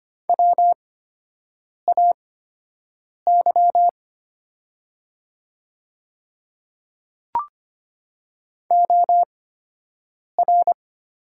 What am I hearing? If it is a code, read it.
Morse code, heard as a keyed tone, about 25 wpm, spaced as for 7 wpm, 700 Hz. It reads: WAY OR
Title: 8x Character Spacing - Top 200 Words - 25wpm